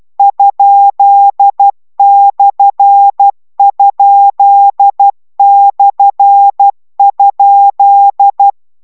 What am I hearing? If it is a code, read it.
Morse code, heard as a keyed tone, about 12 wpm, 800 Hz. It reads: ?/?/?